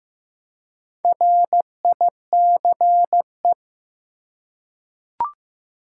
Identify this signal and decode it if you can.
Morse code, heard as a keyed tone, about 15 wpm, 700 Hz. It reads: RICE
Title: Four Letter Words - 15wpm